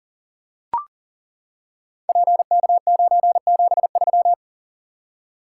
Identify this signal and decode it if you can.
Morse code, heard as a keyed tone, about 39 wpm, 690 Hz. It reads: PK973
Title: Code Groups - 5 Letter:Numbers - 40wpm